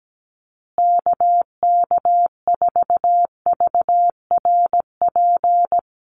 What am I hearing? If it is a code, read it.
Morse code, heard as a keyed tone, about 17 wpm, 700 Hz. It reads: KK4VRP
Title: USA Call Signs Repeated 3x 17wpm